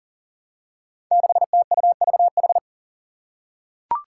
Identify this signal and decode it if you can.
Morse code, heard as a keyed tone, about 39 wpm, 700 Hz. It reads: 6TUVH
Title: Five Challenging Letter-Number Combinations 40wpm